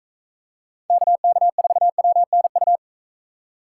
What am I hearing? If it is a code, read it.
Morse code, heard as a keyed tone, about 42 wpm, 700 Hz. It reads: KK4WNU